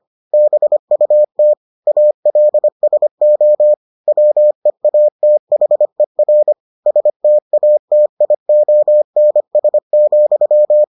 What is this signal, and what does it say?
Morse code, heard as a keyed tone, about 25 wpm, 600 Hz. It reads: BUT ALSO WEATHER STATIONS,